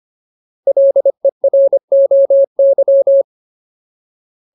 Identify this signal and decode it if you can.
Morse code, heard as a keyed tone, about 25 wpm, 550 Hz. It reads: LEROY
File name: Ultimate Word List / 25WPM Ultimate